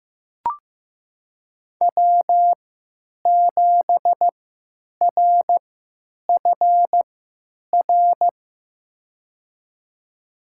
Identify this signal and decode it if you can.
Morse code, heard as a keyed tone, about 15 wpm, spaced as for 9 wpm, 700 Hz. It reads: W7RFR
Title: USA Call Signs - 5 Characters - 3x Character Spacing - 15wpm